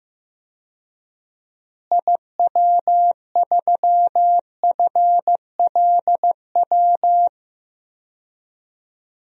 Morse code, heard as a keyed tone, about 15 wpm, 700 Hz. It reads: IW3FLW
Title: DXCC-100 Call Signs - 6 Characters - Repeated 3x - 15wpm